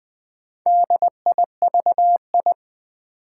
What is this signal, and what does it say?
Morse code, heard as a keyed tone, about 20 wpm, 700 Hz. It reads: DIVI
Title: Build A Word - 2 Letters At A Time - 6-Letter Words - 20wpm